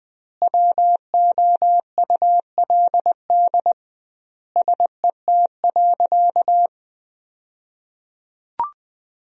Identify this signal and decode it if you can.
Morse code, heard as a keyed tone, about 21 wpm, 700 Hz. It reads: WOULD SET.
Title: 2x Word-Spacing - Sentences from Top 400 Words - 20wpm